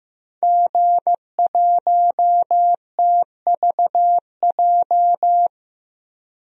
Morse code, heard as a keyed tone, about 15 wpm, 700 Hz. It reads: G1TVJ
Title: DXCC-100 Call Signs - 5 Characters - 15wpm